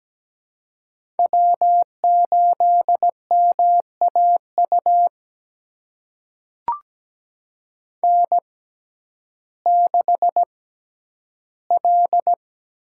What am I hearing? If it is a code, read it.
Morse code, heard as a keyed tone, about 17 wpm, spaced as for 10 wpm, 700 Hz. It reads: W8MAU N6L
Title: USA Call Signs - 5 Characters - 6x Character Spacing - 17wpm